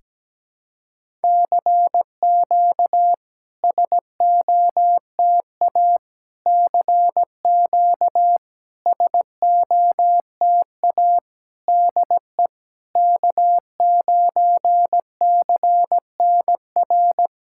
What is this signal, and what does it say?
Morse code, heard as a keyed tone, about 17 wpm, 690 Hz. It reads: CQ SOTA CQ SOTA DE K9CNR
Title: SOTA QSO Practice 17wpm